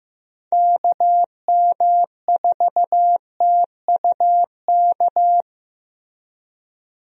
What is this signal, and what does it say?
Morse code, heard as a keyed tone, about 15 wpm, 700 Hz. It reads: KM4TUK
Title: USA Call Signs - 6 Characters - Repeated 3x - 15wpm